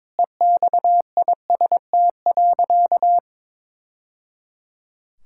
Morse code, heard as a keyed tone, about 22 wpm, 700 Hz. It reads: EXIST.
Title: Sentences from Top 1000 Words - Review Part 6 - 22wpm